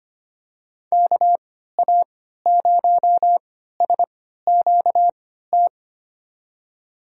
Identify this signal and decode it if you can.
Morse code, heard as a keyed tone, about 25 wpm, spaced as for 14 wpm, 700 Hz. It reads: KA0SQT
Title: USA Call Signs - 6 Characters - 3x Character Spacing - 25wpm